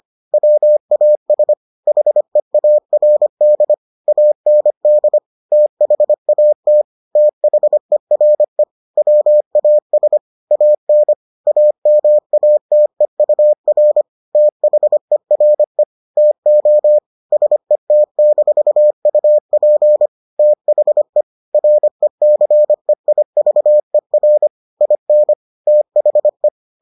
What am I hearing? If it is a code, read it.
Morse code, heard as a keyed tone, about 25 wpm, 600 Hz. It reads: WAS HEARD AND THAT THERE WAS AN AMATEUR THERE TO SET-UP THE RECEIVER IN THE